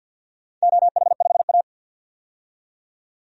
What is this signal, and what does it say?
Morse code, heard as a keyed tone, about 52 wpm, 700 Hz. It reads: OH5A
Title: DXCC-100 Call Signs - 4 Characters - 50wpm